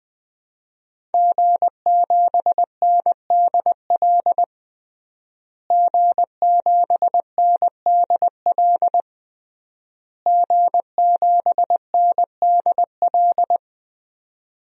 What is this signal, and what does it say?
Morse code, heard as a keyed tone, about 20 wpm, 700 Hz. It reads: G7NDL G7NDL G7NDL
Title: DXCC-100 Call Signs - 5 Characters - Repeated 3x - 20wpm